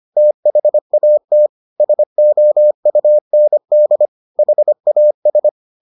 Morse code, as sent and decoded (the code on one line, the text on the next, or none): - .... .- - / ... --- ..- -. -.. / .... .- ...
THAT SOUND HAS